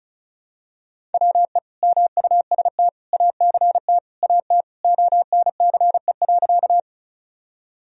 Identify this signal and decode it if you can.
Morse code, heard as a keyed tone, about 36 wpm, 700 Hz. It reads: WE MUST ACT AT ONCE.
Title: Sentences from Top 1000 Words - Review Part 1 - 35wpm